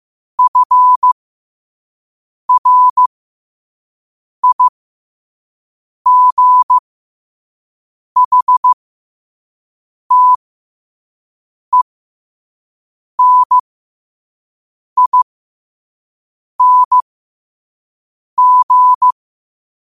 Morse code, heard as a keyed tone, about 15 wpm, spaced as for 5 wpm, 1000 Hz. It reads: FRIGHTENING
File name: qotdfiles / 5wpm